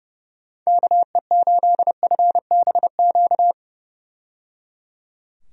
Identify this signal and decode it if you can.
Morse code, heard as a keyed tone, about 30 wpm, 700 Hz. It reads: KE8FBQ
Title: USA Call Signs - 6 Characters - 30wpm